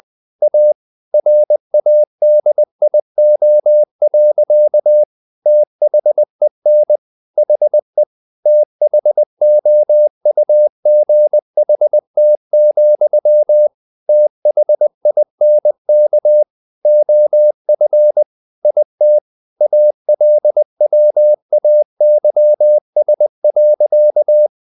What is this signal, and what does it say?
Morse code, heard as a keyed tone, about 20 wpm, 600 Hz. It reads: A RADIO. THEN HE THOUGHT, THINK OF IT ALWAYS.
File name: the old man and the sea / the old man and the sea part 1 20wpm